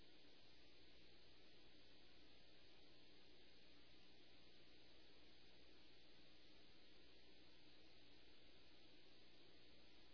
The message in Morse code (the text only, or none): none